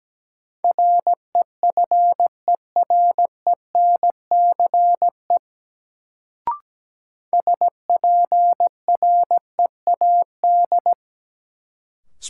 Morse code, heard as a keyed tone, about 17 wpm, 700 Hz. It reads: REFERENCE SPREAD
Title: Sentences from Top 1200 Words - New Word Drill - 17wpm